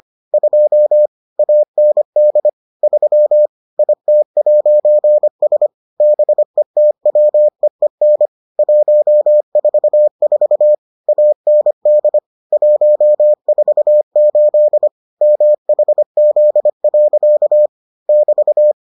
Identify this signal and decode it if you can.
Morse code, heard as a keyed tone, about 25 wpm, 600 Hz. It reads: 2 AND 3 IT'S BETWEEN 144 AND 148 MHZ. <BT>